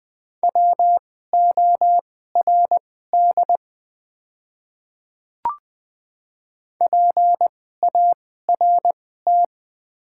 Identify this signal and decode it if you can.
Morse code, heard as a keyed tone, about 20 wpm, spaced as for 14 wpm, 700 Hz. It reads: WORD PART
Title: 2x Character Spacing - Top 200 Words - 20wpm